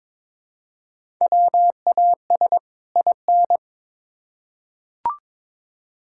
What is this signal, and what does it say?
Morse code, heard as a keyed tone, about 22 wpm, 700 Hz. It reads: WAS IN
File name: Build A Phrase - 2-Word Phrases - 22wpm